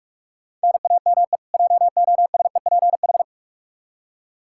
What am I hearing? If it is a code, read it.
Morse code, heard as a keyed tone, about 43 wpm, 700 Hz. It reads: NAME JOSEPH
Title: Practice QSO - Exchange 1 - Beginning - 45wpm